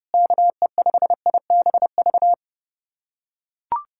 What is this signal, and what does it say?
Morse code, heard as a keyed tone, about 30 wpm, 700 Hz. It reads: KE5IBV
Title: USA Callsign DE Callsign 30wpm